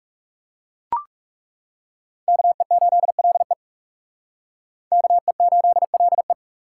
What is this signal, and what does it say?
Morse code, heard as a keyed tone, about 36 wpm, 700 Hz. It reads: KE8LE KE8LE
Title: Speed Racing - USA 2x2 Call Signs 30wpm